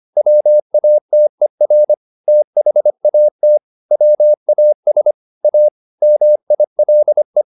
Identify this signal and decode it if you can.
Morse code, heard as a keyed tone, about 25 wpm, 600 Hz. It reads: WATER THAT WAS A MILE
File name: the old man and the sea part 1 25wpm